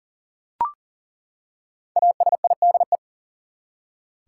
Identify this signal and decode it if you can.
Morse code, heard as a keyed tone, about 40 wpm, 700 Hz. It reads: ASIDE
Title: Five Letter Words - 40wpm